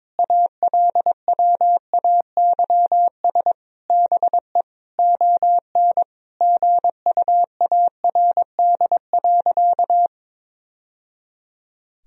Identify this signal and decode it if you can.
Morse code, heard as a keyed tone, about 22 wpm, 700 Hz. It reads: ALWAYS BE ON GUARD.
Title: Sentences from Top 1000 Words - Review Part 4 - 22wpm